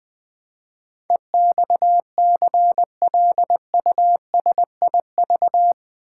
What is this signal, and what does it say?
Morse code, heard as a keyed tone, about 20 wpm, 700 Hz. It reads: EXCLUSIV
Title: Build A Word - 4 Letters At A Time - 11-Letter Words - 20wpm